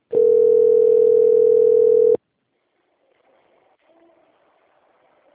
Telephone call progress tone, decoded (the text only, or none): ringback tone